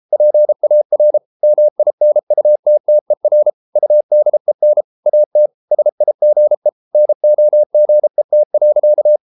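Morse code, heard as a keyed tone, about 33 wpm, 600 Hz. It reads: PAR MINUTTER UDEN AT SIGE NOGET.
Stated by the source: Morse code at 33 wpm, QRQ.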